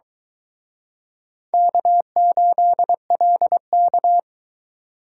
Morse code, heard as a keyed tone, about 22 wpm, 700 Hz. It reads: K8LK